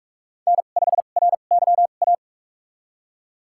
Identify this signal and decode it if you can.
Morse code, heard as a keyed tone, about 43 wpm, spaced as for 32 wpm, 700 Hz. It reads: N5RYA
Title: USA Call Signs - 5 Characters - 2x Character Spacing - 45wpm